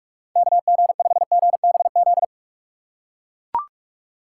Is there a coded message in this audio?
Yes: Morse code from 0.4 to 2.2 s, heard as a keyed tone, about 39 wpm, 700 Hz. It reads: KG5GBZ